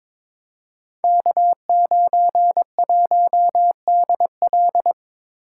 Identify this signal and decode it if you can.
Morse code, heard as a keyed tone, about 23 wpm, 700 Hz. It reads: K91DL
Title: Code Groups - 5 Letter:Numbers - 22wpm